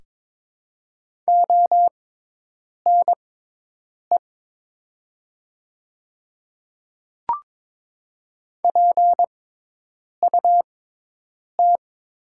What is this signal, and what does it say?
Morse code, heard as a keyed tone, about 22 wpm, spaced as for 8 wpm, 700 Hz. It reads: ONE PUT
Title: 6x Character Spacing - Top 200 Words - 22wpm